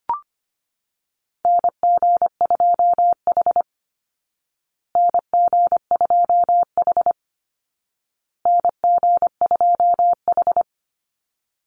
Morse code, heard as a keyed tone, about 25 wpm, 700 Hz. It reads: NG2H NG2H NG2H
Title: Call Signs Repeated 2x - 25wpm